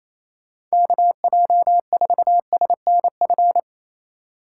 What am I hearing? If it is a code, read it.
Morse code, heard as a keyed tone, about 35 wpm, 700 Hz. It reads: KJ4SNF